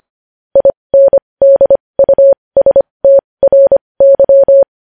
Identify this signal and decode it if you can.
Morse code, heard as a keyed tone, about 25 wpm, spaced as for 20 wpm, 550 Hz. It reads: INDUSTRY
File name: podcast / News headlines-1x-25x20